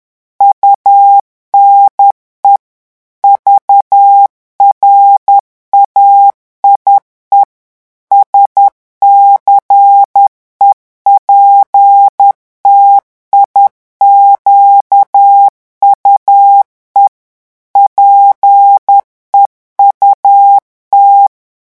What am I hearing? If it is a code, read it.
Morse code, heard as a keyed tone, about 10 wpm, 800 Hz. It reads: UNE VRAIE SCEPTIQUE PEUT
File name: morse mp3 / revision francais 10 mots minute